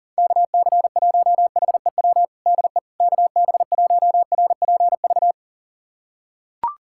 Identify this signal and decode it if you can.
Morse code, heard as a keyed tone, about 40 wpm, 690 Hz. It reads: KC1HEW DE KB1RPV
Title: USA Callsign DE Callsign - 6 Characters - 40wpm